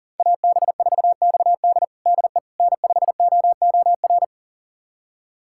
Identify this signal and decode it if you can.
Morse code, heard as a keyed tone, about 40 wpm, 700 Hz. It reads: AB4XD DE N5OOR
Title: USA Callsign DE Callsign - 5 Characters - 40wpm